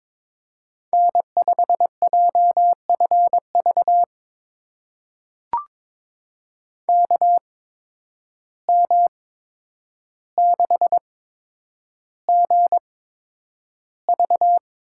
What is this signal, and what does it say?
Morse code, heard as a keyed tone, about 22 wpm, spaced as for 9 wpm, 700 Hz. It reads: N5JFV KM6GV
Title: USA Call Signs - 5 Characters - 8x Character Spacing - 22wpm